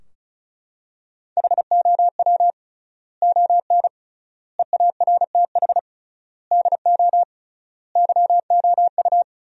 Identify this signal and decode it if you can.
Morse code, heard as a keyed tone, about 36 wpm, 700 Hz. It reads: HOW ON EARTH DO YOU
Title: 3x Word-Spacing - Sentences from Top 200 Words - 35wpm